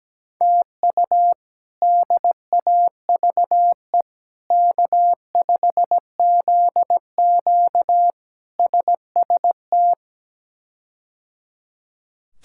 Morse code, heard as a keyed tone, about 17 wpm, 700 Hz. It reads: TU DAVE K5ZQ SST
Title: K1USN SST QSO Practice - 17wpm